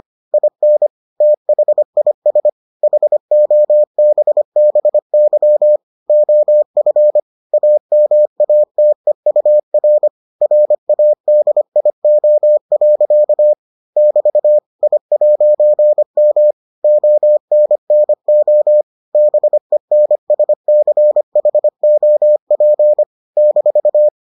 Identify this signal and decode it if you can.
Morse code, heard as a keyed tone, about 25 wpm, 600 Hz. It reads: IN THIS HOBBY OF AMATEUR RADIO. <BT> I'M ONNO BENSCHOP -